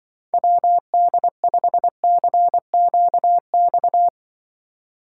Morse code, heard as a keyed tone, about 25 wpm, 700 Hz. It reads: WD5CQX
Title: Speed Racing - USA 2x3 Call Signs 20wpm